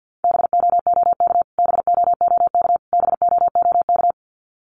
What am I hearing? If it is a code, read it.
Morse code, heard as a keyed tone, about 50 wpm, 700 Hz. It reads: BOOK BOOK BOOK
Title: Top 500 Words Repeated 50wpm